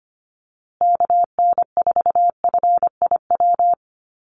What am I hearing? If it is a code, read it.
Morse code, heard as a keyed tone, about 25 wpm, 700 Hz. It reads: KN4FIW
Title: USA Call Signs Repeated 2x 25wpm